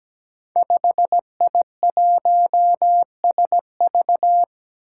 Morse code, heard as a keyed tone, about 17 wpm, 700 Hz. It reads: 5I1SV
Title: Five Challenging Letter-Number Combinations 17wpm